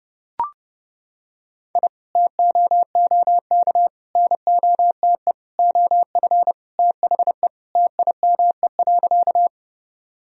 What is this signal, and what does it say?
Morse code, heard as a keyed tone, about 30 wpm, 700 Hz. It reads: I TOOK NOTE OF THE TIME.
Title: Sentences from Top 1000 Words - Review Part 2 - 30wpm